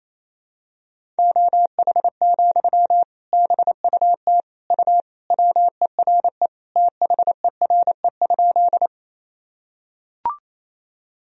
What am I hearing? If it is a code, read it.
Morse code, heard as a keyed tone, about 28 wpm, 700 Hz. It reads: OH, BUT U WERE THERE?
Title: Sentences from Top 500 Words and Abbreviations 28wpm